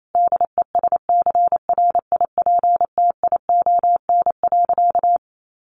Morse code, heard as a keyed tone, about 28 wpm, 700 Hz. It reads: DESCRIPTION.
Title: Sentences from Top 1300 Words 28wpm